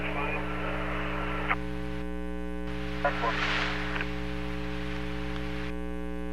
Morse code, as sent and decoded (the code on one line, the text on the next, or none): none